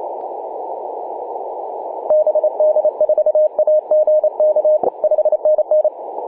Morse code, heard as a keyed tone, about 27 wpm, 610 Hz. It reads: BD4AGK 5NN
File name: posts / oct-19-0756z-vk9xsp-24-901mhz-cw